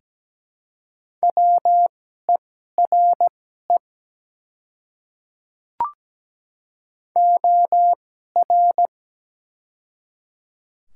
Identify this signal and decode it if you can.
Morse code, heard as a keyed tone, about 17 wpm, spaced as for 12 wpm, 700 Hz. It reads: WERE OR
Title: 2x Character Spacing - Top 200 Words - 17wpm